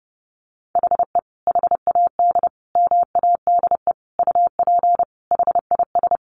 Morse code, heard as a keyed tone, about 30 wpm, 700 Hz. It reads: HE HAD MADE UP HIS
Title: Sentences from Top 500 Words 30wpm